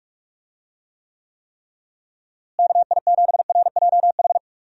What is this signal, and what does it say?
Morse code, heard as a keyed tone, about 41 wpm, 700 Hz. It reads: KI7RJH
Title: USA Call Signs - 6 Characters - Repeated 3x - 45wpm